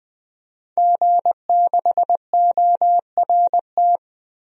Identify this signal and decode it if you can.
Morse code, heard as a keyed tone, about 20 wpm, 700 Hz. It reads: G6ORT
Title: DXCC-100 Call Signs - 5 Characters - 20wpm